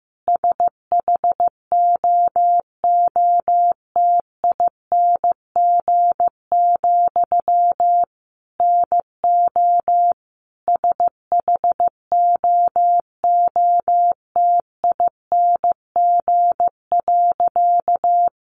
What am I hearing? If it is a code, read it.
Morse code, heard as a keyed tone, about 15 wpm, 700 Hz. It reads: SHOOTING, NO SHOOTING.
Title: A Sound Of Thunder 15wpm